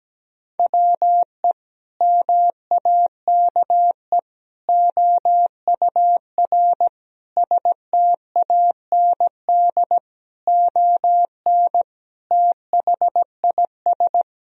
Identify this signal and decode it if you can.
Morse code, heard as a keyed tone, about 17 wpm, 700 Hz. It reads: WE MAKE OUR STAND ON THIS
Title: Sentences from Top 1000 Words - Review Part 2 - 17wpm